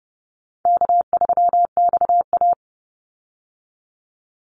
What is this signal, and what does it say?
Morse code, heard as a keyed tone, about 30 wpm, 700 Hz. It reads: K3XA
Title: Call Signs - 30wpm